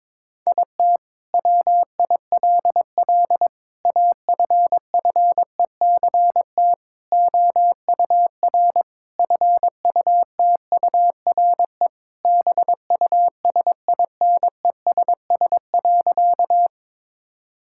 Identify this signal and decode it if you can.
Morse code, heard as a keyed tone, about 22 wpm, 700 Hz. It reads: IT WILL AFFECT OUR FUTURE BUSINESS.
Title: Sentences from Top 1000 Words - Review Part 6 - 22wpm